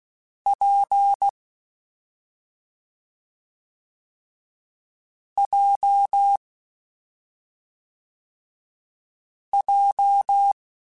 Morse code, heard as a keyed tone, about 16 wpm, 800 Hz. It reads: P J J
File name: Difficult Combos / Difficult Combos-mp3s / WPJ WPJ1 @5